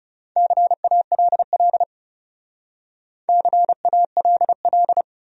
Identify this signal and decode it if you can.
Morse code, heard as a keyed tone, about 39 wpm, 700 Hz. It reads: CALL CALL